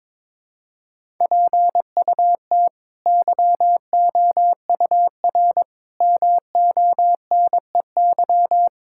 Morse code, heard as a keyed tone, about 22 wpm, 700 Hz. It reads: PUT YOUR MONEY
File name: practice / 22wpm / Sentences with 8-Word Phrases - 22wpm